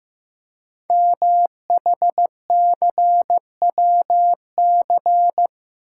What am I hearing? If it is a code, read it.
Morse code, heard as a keyed tone, about 15 wpm, 700 Hz. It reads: MHCWC
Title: Code Groups - 5 Letter:Numbers - 15wpm